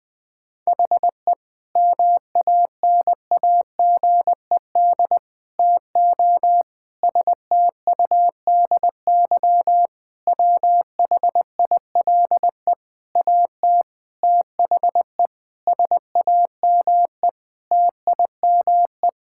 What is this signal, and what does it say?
Morse code, heard as a keyed tone, about 21 wpm, 700 Hz. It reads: HE MANAGED TO STUDY WHILE AT THE SAME TIME
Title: Sentences with 5-Word Phrases - 20wpm